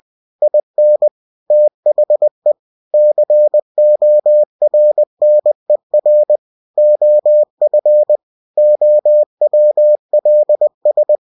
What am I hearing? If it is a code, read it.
Morse code, heard as a keyed tone, about 20 wpm, 600 Hz. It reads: IN THE CORNER OF OWLS
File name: cw-books / winnie-the-pooh / winnie-20wpm-page-0030